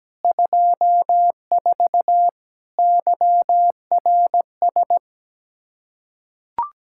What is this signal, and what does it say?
Morse code, heard as a keyed tone, about 17 wpm, 700 Hz. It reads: 24 YRS
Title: K1USN SST QSO Practice 17wpm